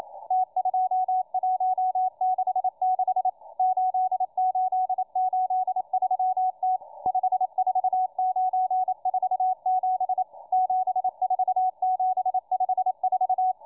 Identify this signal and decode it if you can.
Morse code, heard as a keyed tone, about 28 wpm, 740 Hz, with radio noise. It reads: T2166 8883T 54947 74754